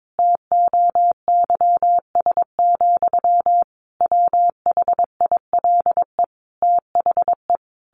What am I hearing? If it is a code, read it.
Morse code, heard as a keyed tone, about 22 wpm, 700 Hz. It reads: TOYS, WHILE THE
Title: A Sound Of Thunder 22wpm